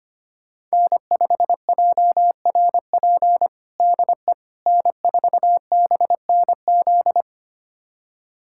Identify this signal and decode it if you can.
Morse code, heard as a keyed tone, about 25 wpm, 700 Hz. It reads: N5JRP DE N4BNZ
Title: USA Callsign DE Callsign - 5 Characters - 25wpm